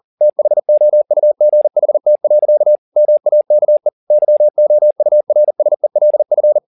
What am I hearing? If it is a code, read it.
Morse code, heard as a keyed tone, about 40 wpm, 600 Hz. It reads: THOUGHT. MAKE YOURSELF
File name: the old man and the sea 2 40wpm